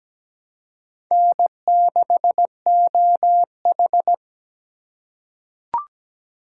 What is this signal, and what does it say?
Morse code, heard as a keyed tone, about 17 wpm, 700 Hz. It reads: N6OH
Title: USA Call Signs - 4 Characters - 4x Character Spacing - 17wpm